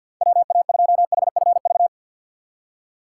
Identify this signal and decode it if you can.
Morse code, heard as a keyed tone, about 47 wpm, 700 Hz. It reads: WA2HFV
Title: USA Call Signs - 2x Character Spacing - 50wpm